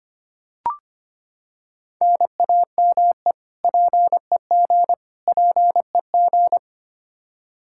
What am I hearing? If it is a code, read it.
Morse code, heard as a keyed tone, about 25 wpm, 700 Hz. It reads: NAME PEG PEG
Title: Practice QSO - Exchange 1 - Beginning - 25wpm